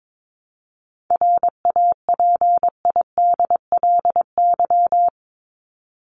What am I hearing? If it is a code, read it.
Morse code, heard as a keyed tone, about 22 wpm, 700 Hz. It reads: RAPIDLY
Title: LY Suffix - Top 100 Words with Base Words 22wpm